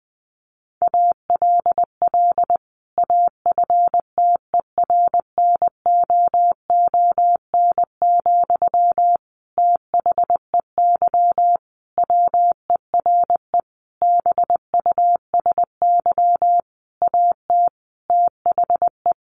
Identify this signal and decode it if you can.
Morse code, heard as a keyed tone, about 20 wpm, 700 Hz. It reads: ALL AFTERNOON, THEY WERE BUSY AT THE
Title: Sentences from Top 1500 Words 20wpm